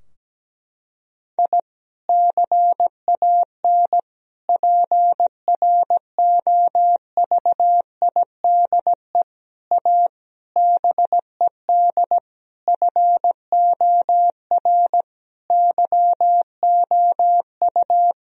Morse code, heard as a keyed tone, about 17 wpm, 700 Hz. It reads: I CAN PROVIDE A BED FOR YOU
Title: Sentences from Top 1000 Words - Review Part 3 - 17wpm